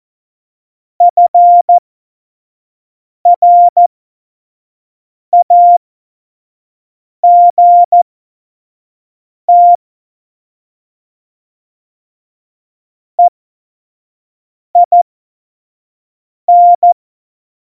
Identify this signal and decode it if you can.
Morse code, heard as a keyed tone, about 14 wpm, spaced as for 5 wpm, 700 Hz. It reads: FRAGT EIN
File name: Morsekurs / Witze / Witz 22. Mai 2018, 25 F70